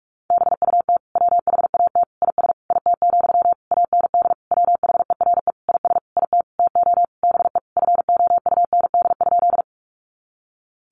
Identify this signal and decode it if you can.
Morse code, heard as a keyed tone, about 44 wpm, 700 Hz. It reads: BUT WHAT IS IT, AND WHERE IS IT TO BE FOUND?
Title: Sentences from Top 300 Words 45wpm